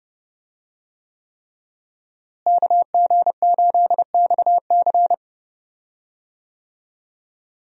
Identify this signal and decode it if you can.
Morse code, heard as a keyed tone, about 28 wpm, 700 Hz. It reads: KG8XC
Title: USA Call Signs - 5 Characters - Repeated 3x - 30wpm